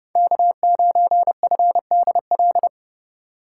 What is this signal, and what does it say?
Morse code, heard as a keyed tone, about 35 wpm, 700 Hz. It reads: K9FDL
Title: Speed Racing - USA 1x3 Call Signs 30wpm